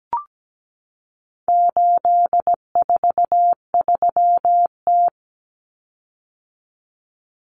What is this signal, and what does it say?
Morse code, heard as a keyed tone, about 17 wpm, 700 Hz. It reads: 843T
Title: Common Cut Numbers - 4-Digit 17wpm